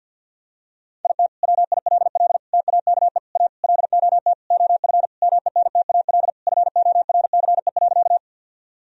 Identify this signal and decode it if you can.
Morse code, heard as a keyed tone, about 52 wpm, 700 Hz. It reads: IT WILL TAKE A LOT OF MENTAL FORCE.
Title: Sentences from Top 1000 Words - Review Part 4 - 50wpm